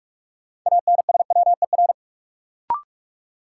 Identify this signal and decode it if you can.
Morse code, heard as a keyed tone, about 41 wpm, 700 Hz. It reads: ANSWER
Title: Sentences from Top 300 Words - New Word Drill - 45wpm